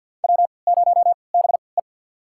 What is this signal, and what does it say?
Morse code, heard as a keyed tone, about 47 wpm, spaced as for 28 wpm, 700 Hz. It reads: W0BE
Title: USA Call Signs - 3x Character Spacing - 50wpm